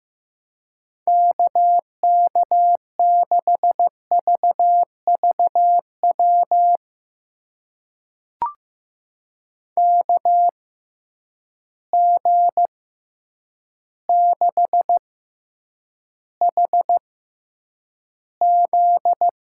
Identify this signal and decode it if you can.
Morse code, heard as a keyed tone, about 15 wpm, spaced as for 8 wpm, 700 Hz. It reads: KK6VVW KG6HZ
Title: USA Call Signs - 6 Characters - 6x Character Spacing - 15wpm